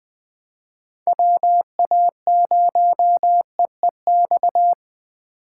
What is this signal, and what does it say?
Morse code, heard as a keyed tone, about 20 wpm, 700 Hz. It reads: WA0EEX
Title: USA Call Signs - 6 Characters - 4x Character Spacing - 20wpm